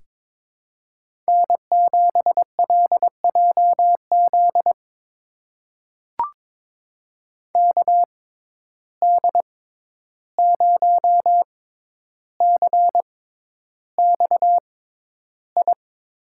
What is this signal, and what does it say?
Morse code, heard as a keyed tone, about 22 wpm, spaced as for 11 wpm, 700 Hz. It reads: N7LJZ KD0CXI
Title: USA Call Signs - 6x Character Spacing - 22wpm